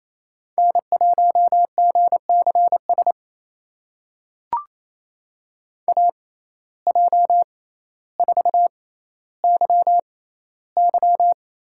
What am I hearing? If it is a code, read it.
Morse code, heard as a keyed tone, about 28 wpm, spaced as for 14 wpm, 700 Hz. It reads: N1GCS AJ4YY
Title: USA Call Signs - 6x Character Spacing - 28wpm